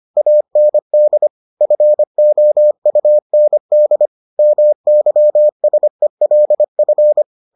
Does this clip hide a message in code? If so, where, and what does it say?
Yes: Morse code from 0.2 to 7.2 s, heard as a keyed tone, about 25 wpm, 600 Hz. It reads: AND FOUND MYSELF